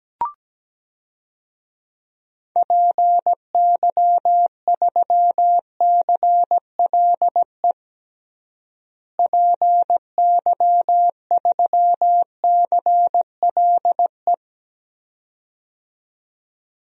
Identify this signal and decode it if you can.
Morse code, heard as a keyed tone, about 17 wpm, 700 Hz. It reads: PY3CLE PY3CLE
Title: DXCC-100 Call Signs - 6 Characters - Repeated 2x - 17wpm